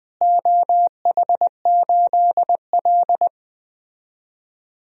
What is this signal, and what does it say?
Morse code, heard as a keyed tone, about 20 wpm, 700 Hz. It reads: OH8L
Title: DXCC-100 Call Signs - 4 Characters - Repeated 3x - 20wpm